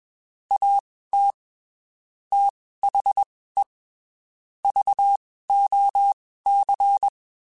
Morse code, heard as a keyed tone, about 21 wpm, spaced as for 15 wpm, 800 Hz. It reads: AT THE VOC
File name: (spaced) / QSOs Spaced-mp3s / WD0BGP 1 @20 sp